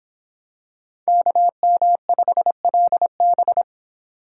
Morse code, heard as a keyed tone, about 25 wpm, 700 Hz. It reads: KM5LB